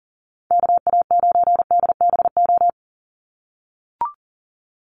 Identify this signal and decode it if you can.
Morse code, heard as a keyed tone, about 40 wpm, 690 Hz. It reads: KA9DBO
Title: USA Call Signs 40wpm